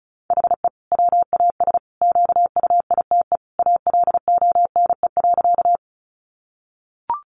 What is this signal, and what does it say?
Morse code, heard as a keyed tone, about 34 wpm, 700 Hz. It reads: HE WAS QUITE ALONE.
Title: Sentences from Top 1000 Words 35wpm